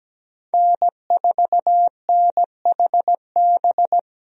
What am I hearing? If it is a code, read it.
Morse code, heard as a keyed tone, about 17 wpm, 700 Hz. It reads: N4NHB
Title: USA Call Signs - 5 Characters - 17wpm